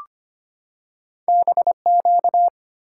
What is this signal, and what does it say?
Morse code, heard as a keyed tone, about 25 wpm, 700 Hz. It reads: BQ
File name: Two Letter-Number Combinations - 25wpm